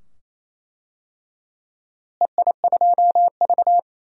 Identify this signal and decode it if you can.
Morse code, heard as a keyed tone, about 27 wpm, 700 Hz. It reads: EI2V
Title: DXCC-100 Call Signs - 4 Characters - Repeated 2x - 28wpm